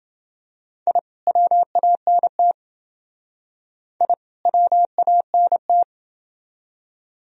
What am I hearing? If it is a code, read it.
Morse code, heard as a keyed tone, about 29 wpm, 700 Hz. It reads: I WANT I WANT